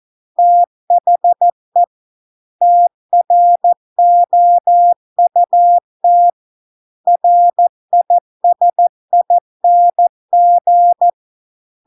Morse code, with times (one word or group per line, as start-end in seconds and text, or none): 0.4-1.8 THE
2.6-6.3 TROUT
7.1-11.1 RISING